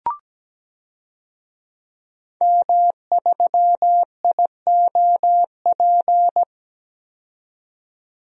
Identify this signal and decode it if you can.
Morse code, heard as a keyed tone, about 17 wpm, 700 Hz. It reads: M3IOP
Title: DXCC-100 Call Signs - 5 Characters - 17wpm